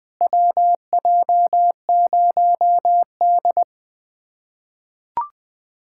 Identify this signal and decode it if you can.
Morse code, heard as a keyed tone, about 20 wpm, 700 Hz. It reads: WJ0D
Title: USA Callsign DE Callsign - 4 Characters - 20wpm